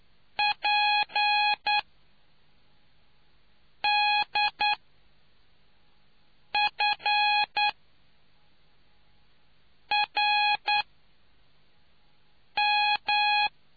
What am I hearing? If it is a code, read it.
Morse code, heard as a keyed tone, about 9 wpm, spaced as for 4 wpm, 830 Hz. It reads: PDFRM